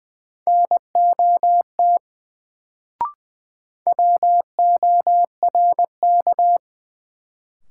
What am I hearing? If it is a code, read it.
Morse code, heard as a keyed tone, about 20 wpm, 700 Hz. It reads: NOT WORK
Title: Sets of 2 Words - New Word Drill - 20wpm